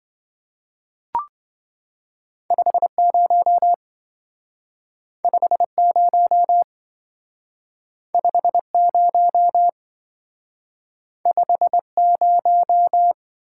Morse code, heard as a keyed tone, about 27 wpm, 700 Hz. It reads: 50 50 50 50